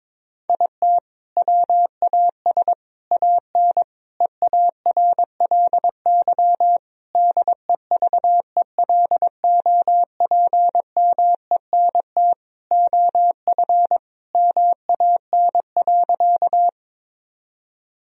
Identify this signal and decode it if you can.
Morse code, heard as a keyed tone, about 22 wpm, 700 Hz. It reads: IT WAS AN EARLY DEVELOPMENT OF MAN.
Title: Sentences from Top 1000 Words - Review Part 4 - 22wpm